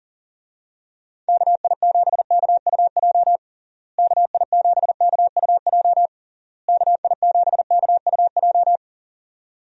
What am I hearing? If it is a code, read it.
Morse code, heard as a keyed tone, about 40 wpm, 700 Hz. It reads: KI7KUJ KI7KUJ KI7KUJ
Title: USA Call Signs - 6 Characters - Repeated 3x - 40wpm